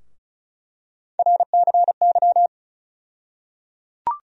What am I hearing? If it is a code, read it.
Morse code, heard as a keyed tone, about 36 wpm, 700 Hz. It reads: RCY
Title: USA Call Sign Postfixes - 3 Letters 35wpm